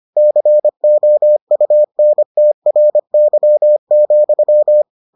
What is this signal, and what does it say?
Morse code, heard as a keyed tone, about 25 wpm, 600 Hz. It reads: COUNTRY,